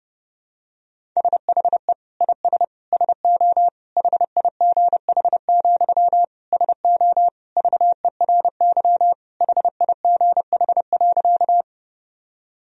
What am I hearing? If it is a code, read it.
Morse code, heard as a keyed tone, about 30 wpm, 700 Hz. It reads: SHE IS SO HIGH, SO VERY HIGH.
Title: Sentences from Top 1000 Words - Review Part 1 - 30wpm